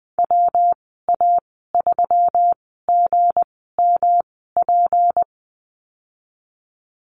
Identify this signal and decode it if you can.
Morse code, heard as a keyed tone, about 20 wpm, spaced as for 15 wpm, 700 Hz. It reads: WA3GMP
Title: 186 Lesson 30g - Letter b - Call Signs 20wpm